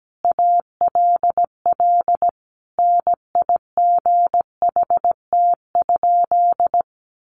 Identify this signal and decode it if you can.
Morse code, heard as a keyed tone, about 17 wpm, 700 Hz. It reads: ALL NIGHT?
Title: Sentences from Top 700 Words 17wpm